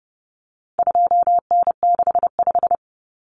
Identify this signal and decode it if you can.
Morse code, heard as a keyed tone, about 30 wpm, 700 Hz. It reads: 2N65
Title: Common Cut Numbers - 4-Digit 30wpm